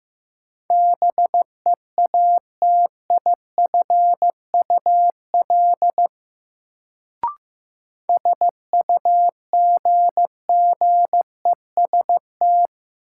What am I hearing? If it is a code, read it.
Morse code, heard as a keyed tone, about 15 wpm, 700 Hz. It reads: BEATIFUL SUGGEST
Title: Sentences from Top 1300 Words - New Word Drill - 15wpm